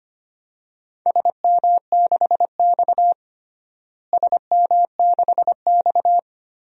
Morse code, heard as a keyed tone, about 25 wpm, 700 Hz. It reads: SM6X SM6X
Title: DXCC-100 Call Signs - 4 Characters - Repeated 2x - 25wpm